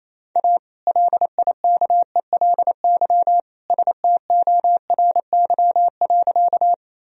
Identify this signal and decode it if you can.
Morse code, heard as a keyed tone, about 28 wpm, 700 Hz. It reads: A LIKELY STORY.
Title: Sentences from Top 1000 Words - Review Part 4 - 28wpm